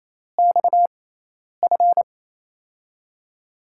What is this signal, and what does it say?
Morse code, heard as a keyed tone, about 28 wpm, spaced as for 10 wpm, 700 Hz. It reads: XF
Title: USA Call Signs - 4 Characters - 6x Character Spacing - 28wpm